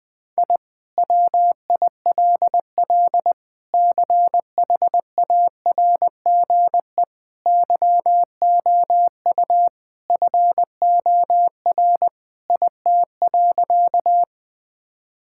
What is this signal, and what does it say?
Morse code, heard as a keyed tone, about 20 wpm, 700 Hz. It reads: I WILL CHARGE YOU FOR IT.
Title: Sentences from Top 1000 Words - Review Part 6 - 20wpm